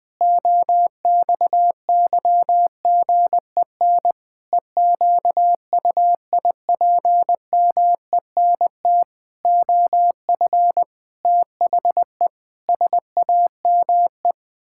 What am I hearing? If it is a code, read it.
Morse code, heard as a keyed tone, about 20 wpm, 700 Hz. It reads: OXYGEN EQUIPMENT OF THE SAME
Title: Pangram Challenge v2 - 20wpm